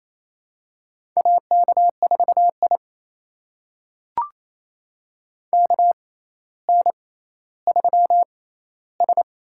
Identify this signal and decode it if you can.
Morse code, heard as a keyed tone, about 27 wpm, spaced as for 14 wpm, 700 Hz. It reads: AK4I KN3S
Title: USA Call Signs - 4 Characters - 6x Character Spacing - 28wpm